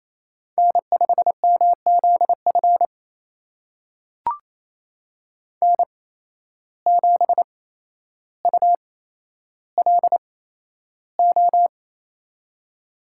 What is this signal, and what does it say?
Morse code, heard as a keyed tone, about 28 wpm, spaced as for 12 wpm, 700 Hz. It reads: N5MZF N7ULO
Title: USA Call Signs - 5 Characters - 8x Character Spacing - 28wpm